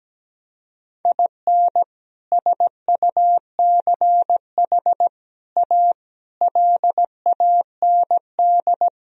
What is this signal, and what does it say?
Morse code, heard as a keyed tone, about 17 wpm, 700 Hz. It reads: IN SUCH A LAND
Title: Sentences from Top 1000 Words - Review Part 1 - 17wpm